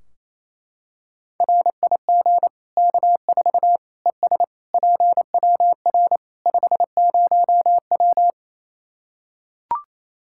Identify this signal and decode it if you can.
Morse code, heard as a keyed tone, about 28 wpm, 700 Hz. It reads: RIG K4 ES PWR 50W
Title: Practice QSO - Exchange 2 - Info - 28wpm